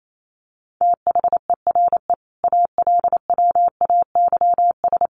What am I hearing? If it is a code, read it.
Morse code, heard as a keyed tone, about 28 wpm, 700 Hz. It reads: THERE ALWAYS
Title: Sentences from Top 1100 Words 28wpm